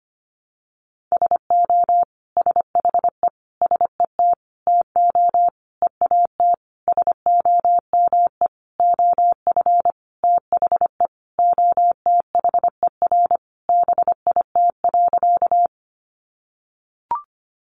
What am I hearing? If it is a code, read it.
Morse code, heard as a keyed tone, about 25 wpm, 700 Hz. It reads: SO SHE SET TO EAT SOME OF THE OTHER BIT.
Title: Sentences from Top 900 Words 25wpm